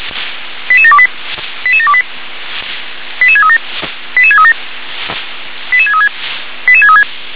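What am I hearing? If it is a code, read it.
Five-tone selective call (ZVEI), heard as quick runs of five tones, tones of about 70 ms. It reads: 88728 88728 88637 88637 88636 88636
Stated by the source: ZVEI five-tone selective call 88730 received by SDR.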